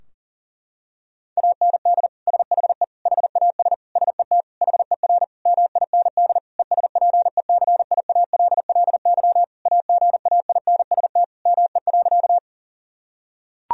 Morse code, heard as a keyed tone, about 40 wpm, 700 Hz. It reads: AND SHE HAS SET HER MIND ESPECIALLY AGAINST ME.
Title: Sentences from Top 1000 Words - Review Part 4 - 40wpm